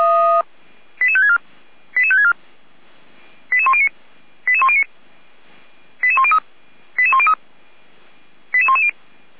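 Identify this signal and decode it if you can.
Five-tone selective call (ZVEI), heard as quick runs of five tones, tones of about 70 ms. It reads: 88564 88564 88109 88109 88103 88103 88110
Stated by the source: ZVEI five-tone selective call received on an SDR.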